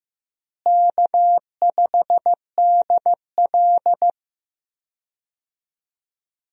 Morse code, heard as a keyed tone, about 15 wpm, 700 Hz. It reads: K5DL
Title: USA Call Signs - 4 Characters - Repeated 3x - 15wpm